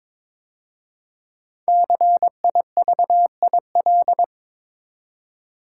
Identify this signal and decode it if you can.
Morse code, heard as a keyed tone, about 22 wpm, 700 Hz. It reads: CIVIL
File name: 5 Letter Words - First 3 Letters Given - 22wpm